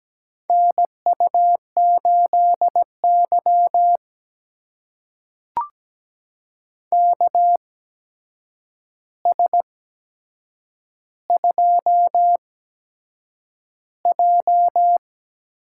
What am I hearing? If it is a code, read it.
Morse code, heard as a keyed tone, about 17 wpm, spaced as for 7 wpm, 700 Hz. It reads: NU8Y KS2J
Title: USA Call Signs - 4 Characters - 8x Character Spacing - 17wpm